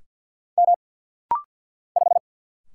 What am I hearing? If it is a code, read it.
Morse code, heard as a keyed tone, about 49 wpm, 700 Hz. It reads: M 5
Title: Single Letter-Number - Rapid-Fire 50wpm